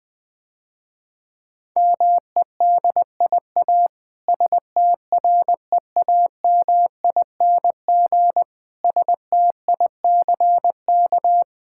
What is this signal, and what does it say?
Morse code, heard as a keyed tone, about 20 wpm, 700 Hz. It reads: MEDIA STREAMING STICK
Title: Difficult Words - Speed-Racing with Hints - 20wpm